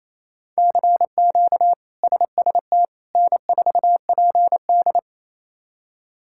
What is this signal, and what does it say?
Morse code, heard as a keyed tone, about 27 wpm, 700 Hz. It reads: CQ SST N4PD
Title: K1USN SST QSO Practice - 28wpm